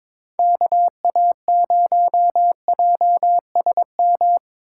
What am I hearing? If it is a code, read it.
Morse code, heard as a keyed tone, about 26 wpm, 700 Hz. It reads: KA0JSM
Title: Speed Racing - USA 2x3 Call Signs 22wpm